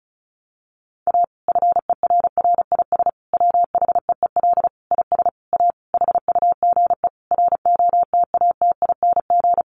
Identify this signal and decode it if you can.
Morse code, heard as a keyed tone, about 34 wpm, 700 Hz. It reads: A FERRIS WHEEL IS A HUGE ROTATING
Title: Additional Easy Phrases 35wpm